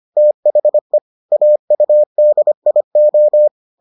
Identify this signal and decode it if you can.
Morse code, heard as a keyed tone, about 25 wpm, 600 Hz. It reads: THE AUDIO